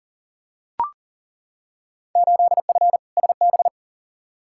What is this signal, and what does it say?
Morse code, heard as a keyed tone, about 38 wpm, 700 Hz. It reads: 8F SB
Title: Field Day Exchanges Encore - 40wpm